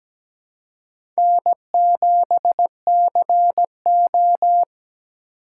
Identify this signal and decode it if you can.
Morse code, heard as a keyed tone, about 17 wpm, 700 Hz. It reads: N7CO
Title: USA Call Signs - 4 Characters - 2x Character Spacing - 17wpm